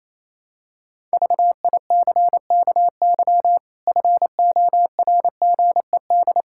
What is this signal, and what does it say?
Morse code, heard as a keyed tone, about 28 wpm, 700 Hz. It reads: VICKY FORGED
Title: Pangram Challenge v2 - 28wpm